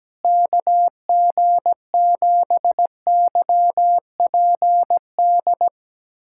Morse code, heard as a keyed tone, about 17 wpm, 700 Hz. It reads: KG7YPD
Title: USA Callsign DE Callsign 17wpm